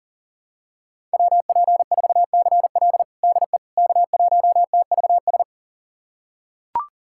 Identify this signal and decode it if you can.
Morse code, heard as a keyed tone, about 40 wpm, 700 Hz. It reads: WP4CL DE K1TVS
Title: USA Callsign DE Callsign - 5 Characters - 40wpm